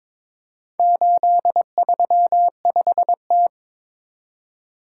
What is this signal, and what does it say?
Morse code, heard as a keyed tone, about 23 wpm, 700 Hz. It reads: 835T
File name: practice / 22wpm / Four Letter-Number Combinations - 22wpm